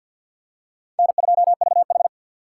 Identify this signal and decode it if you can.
Morse code, heard as a keyed tone, about 53 wpm, 700 Hz. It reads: N2VH
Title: USA Call Signs - 4 Characters - 3x Character Spacing - 50wpm